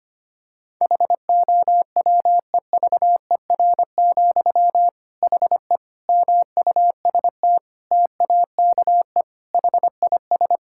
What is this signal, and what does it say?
Morse code, heard as a keyed tone, about 25 wpm, 700 Hz. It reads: HOWEVER, HE MUST TAKE HIS
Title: Sentences from Top 1600 Words 25wpm